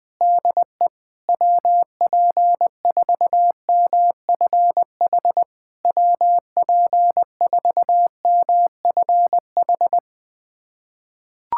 Morse code, heard as a keyed tone, about 20 wpm, 690 Hz. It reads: DE WP4MFH WP4MFH
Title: SOTA QSO Practice 20wpm